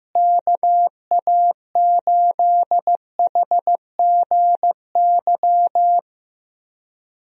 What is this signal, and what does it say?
Morse code, heard as a keyed tone, about 15 wpm, 700 Hz. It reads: KA8HGY
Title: USA Call Signs - 6 Characters - Repeated 2x - 15wpm